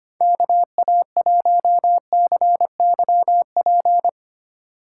Morse code, heard as a keyed tone, about 25 wpm, 700 Hz. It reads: KA1CYP
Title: USA Call Signs - 3x Character Spacing - 25wpm